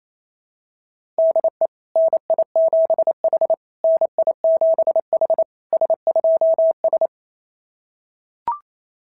Morse code, heard as a keyed tone, about 28 wpm, 690 Hz. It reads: DE NI7H NI7H S2S
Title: SOTA QSO Practice 28wpm